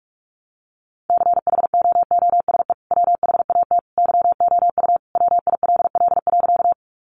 Morse code, heard as a keyed tone, about 44 wpm, 700 Hz. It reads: CHOOSE WHAT YOU WILL.
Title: Sentences from Top 700 Words 45wpm